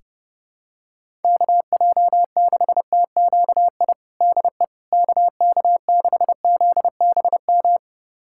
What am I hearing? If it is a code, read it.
Morse code, heard as a keyed tone, about 30 wpm, 700 Hz. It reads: KJ6TQI DE KK6ZBM
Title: USA Callsign DE Callsign 30wpm